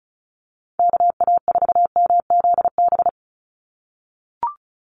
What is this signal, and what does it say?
Morse code, heard as a keyed tone, about 34 wpm, 700 Hz. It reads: KA4MZB
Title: USA Call Signs 35wpm